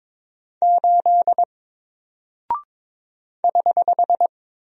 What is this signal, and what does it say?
Morse code, heard as a keyed tone, about 22 wpm, 700 Hz. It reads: 8 <HH>
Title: Single Character (Letter:Number:Punctuation:Prosign) 22wpm